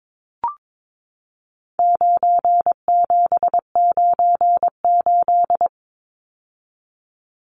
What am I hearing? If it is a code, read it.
Morse code, heard as a keyed tone, about 22 wpm, 700 Hz. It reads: 9798
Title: Numbers - 4-Digit 22wpm